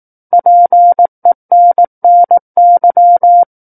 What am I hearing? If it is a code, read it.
Morse code, heard as a keyed tone, about 18 wpm, 700 Hz. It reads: PENNY
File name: 18 WPM